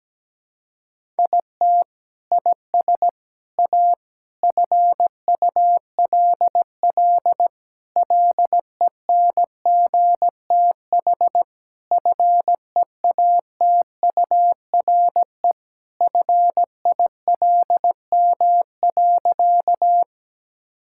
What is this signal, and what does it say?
Morse code, heard as a keyed tone, about 17 wpm, 700 Hz. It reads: IT IS A FULL LENGTH FEATURE FILM.
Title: Sentences from Top 1000 Words - Review Part 5 - 17wpm